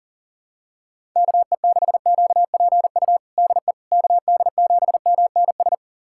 Morse code, heard as a keyed tone, about 40 wpm, 700 Hz. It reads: KE6QPU DE KD7MNS
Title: USA Callsign DE Callsign 40wpm